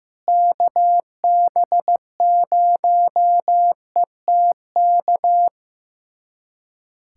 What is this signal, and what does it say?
Morse code, heard as a keyed tone, about 15 wpm, 700 Hz. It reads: KB0ETK
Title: USA Callsign DE Callsign - 6 Characters - 15wpm